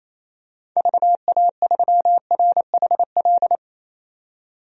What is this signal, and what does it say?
Morse code, heard as a keyed tone, about 28 wpm, 700 Hz. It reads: VA3RHL
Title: DXCC-100 Call Signs - 6 Characters - 28wpm